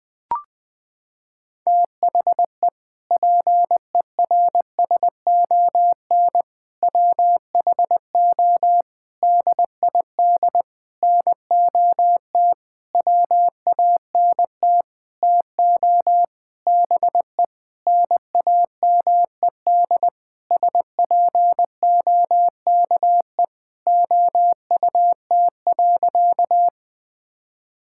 Morse code, heard as a keyed tone, about 21 wpm, 700 Hz. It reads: THE PERSON WHO DID NOT WANT TO BE NAMED SPOKE OUT.
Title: Sentences with 7-Word Phrases - 20wpm